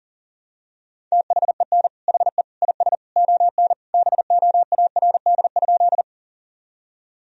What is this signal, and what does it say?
Morse code, heard as a keyed tone, about 40 wpm, 700 Hz. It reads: THEN HE IS ON BOARD?
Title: Sentences from Top 1000 Words - Review Part 4 - 40wpm